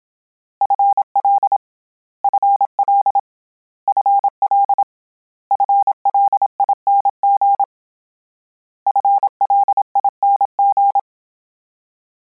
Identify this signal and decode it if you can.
Morse code, heard as a keyed tone, about 27 wpm, 800 Hz. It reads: FL FL FL FLING FLING